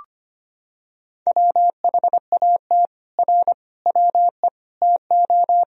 Morse code, heard as a keyed tone, about 25 wpm, 700 Hz. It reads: WHAT R WE TO
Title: Sentences from Top 100 Words and Abbreviations 25wpm